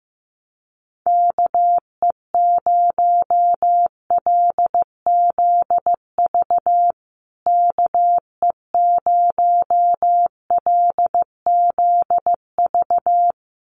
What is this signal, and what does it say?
Morse code, heard as a keyed tone, about 15 wpm, 700 Hz. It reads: KE0LZV KE0LZV
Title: USA Call Signs Repeated 2x 15wpm